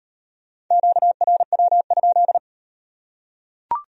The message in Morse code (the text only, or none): QRW?